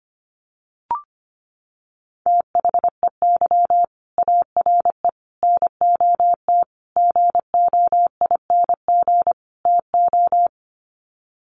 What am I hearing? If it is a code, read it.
Morse code, heard as a keyed tone, about 25 wpm, 700 Hz. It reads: THEY ARE NOT GOING TO
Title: Sets of 5 Words Encore2 25wpm